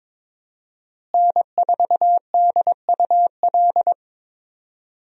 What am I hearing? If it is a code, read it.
Morse code, heard as a keyed tone, about 21 wpm, 700 Hz. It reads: N4DUL
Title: USA Call Signs - 5 Characters - Repeated 3x - 22wpm